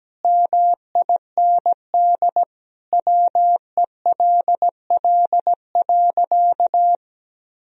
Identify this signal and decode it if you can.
Morse code, heard as a keyed tone, about 17 wpm, 700 Hz. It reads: MIND WELL.
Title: Sentences from Top 1000 Words - Review Part 2 - 17wpm